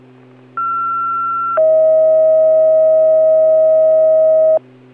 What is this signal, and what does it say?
Two-tone page, heard as a one-second tone, then a three-second tone, about 1360 then 640 Hz. It is heard from 0.6 to 4.6 s.